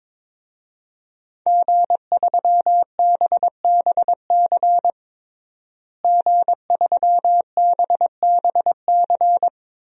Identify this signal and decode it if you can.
Morse code, heard as a keyed tone, about 22 wpm, 700 Hz. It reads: G3BBC G3BBC
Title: DXCC-100 Call Signs - 5 Characters - Repeated 3x - 22wpm